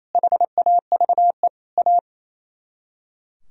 Morse code, heard as a keyed tone, about 33 wpm, 700 Hz. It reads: HAVE A
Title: Speed-Racing - Sets of 3 Words 28wpm